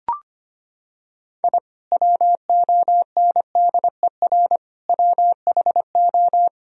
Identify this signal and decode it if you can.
Morse code, heard as a keyed tone, about 25 wpm, 700 Hz. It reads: I WONDER WHO
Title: Sentences from Top 1000 Words - Review Part 3 - 25wpm